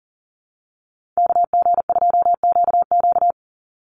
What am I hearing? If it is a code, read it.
Morse code, heard as a keyed tone, about 40 wpm, 690 Hz. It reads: KG2QQ
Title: USA Call Signs 40wpm